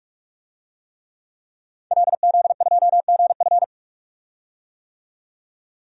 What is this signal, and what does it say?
Morse code, heard as a keyed tone, about 46 wpm, 700 Hz. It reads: LZ2GF
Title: DXCC-100 Call Signs - 5 Characters - 45wpm